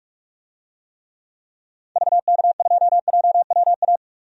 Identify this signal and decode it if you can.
Morse code, heard as a keyed tone, about 45 wpm, 700 Hz. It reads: VK2JWA
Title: DXCC-100 Call Signs - 6 Characters - 45wpm